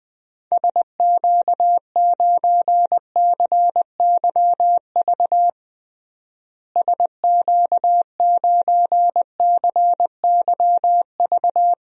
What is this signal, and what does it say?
Morse code, heard as a keyed tone, about 21 wpm, 700 Hz. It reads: SQ9CYV SQ9CYV
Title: DXCC-100 Call Signs - 6 Characters - Repeated 3x - 20wpm